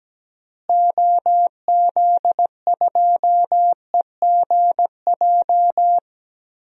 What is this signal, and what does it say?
Morse code, heard as a keyed tone, about 17 wpm, 700 Hz. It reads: OZ2EGJ
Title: DXCC-100 Call Signs - 6 Characters - Repeated 2x - 17wpm